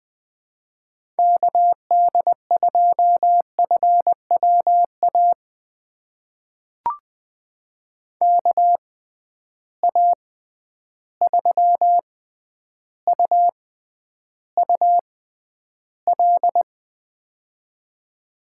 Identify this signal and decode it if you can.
Morse code, heard as a keyed tone, about 20 wpm, spaced as for 10 wpm, 700 Hz. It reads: KD2FWA KA3UUL
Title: USA Call Signs - 6 Characters - 6x Character Spacing - 20wpm